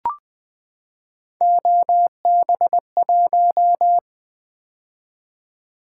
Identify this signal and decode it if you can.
Morse code, heard as a keyed tone, about 20 wpm, 700 Hz. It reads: OB1
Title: Three Letter-Number Combinations - 20wpm